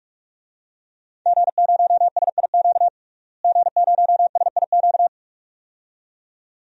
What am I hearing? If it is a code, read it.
Morse code, heard as a keyed tone, about 46 wpm, 700 Hz. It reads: G0SIQ G0SIQ
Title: DXCC-100 Call Signs - 5 Characters - Repeated 2x - 45wpm